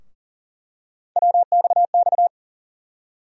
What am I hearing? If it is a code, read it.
Morse code, heard as a keyed tone, about 40 wpm, 700 Hz. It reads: WXX
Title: USA Call Sign Postfixes - 3 Letters 40wpm